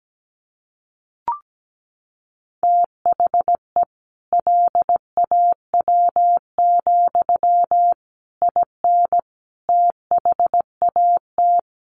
Morse code, heard as a keyed tone, about 17 wpm, 700 Hz. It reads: THE LAW, IN THAT
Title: Sentences from Top 1400 Words 17wpm